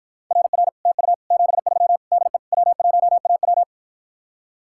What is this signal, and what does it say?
Morse code, heard as a keyed tone, about 53 wpm, 700 Hz. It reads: RR TU 73 DE W1AW